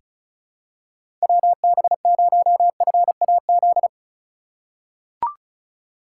Morse code, heard as a keyed tone, about 36 wpm, 700 Hz. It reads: WB0FAZ
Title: USA Call Signs - 6 Characters - 2x Character Spacing - 35wpm